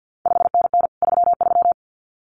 Morse code, heard as a keyed tone, about 50 wpm, 700 Hz. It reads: 5NN 33
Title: CQ WW DX Contest Exchange - 50wpm